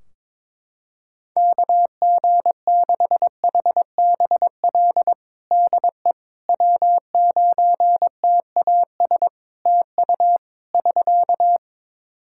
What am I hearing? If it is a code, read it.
Morse code, heard as a keyed tone, about 22 wpm, 700 Hz. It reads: KG6HBL DE W9TAS TU <SK>
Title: Practice QSO - Exchange 4 - Ending - 22wpm